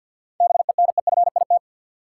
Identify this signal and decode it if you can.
Morse code, heard as a keyed tone, about 53 wpm, 700 Hz. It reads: BENEFIT
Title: Sentences from Top 1000 Words - Review Part 6 - New Word Drill - 50wpm